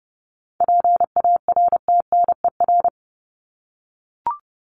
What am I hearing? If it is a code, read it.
Morse code, heard as a keyed tone, about 30 wpm, 700 Hz. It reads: PARTNER
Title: Top 1501-1750 Words 30wpm